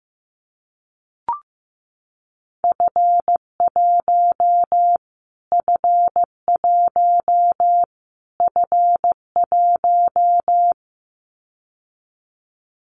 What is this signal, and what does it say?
Morse code, heard as a keyed tone, about 15 wpm, 700 Hz. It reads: F1 F1 F1
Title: Callsign Prefixes for Top 25 DXCC - Repeated 3x - 15wpm